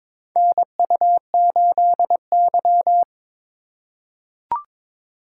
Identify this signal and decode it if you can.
Morse code, heard as a keyed tone, about 22 wpm, 700 Hz. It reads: NU8Y
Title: USA Call Signs - 3x Character Spacing - 22wpm